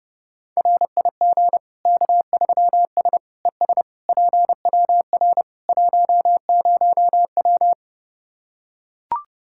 Morse code, heard as a keyed tone, about 30 wpm, 700 Hz. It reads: RIG K3S ES PWR 10W
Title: Practice QSO - Exchange 2 - Info - 30wpm